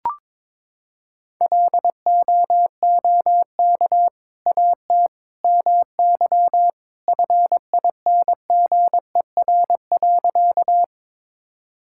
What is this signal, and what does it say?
Morse code, heard as a keyed tone, about 22 wpm, 700 Hz. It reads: LOOK AT MY FINGER.
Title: Sentences from Top 1600 Words 22wpm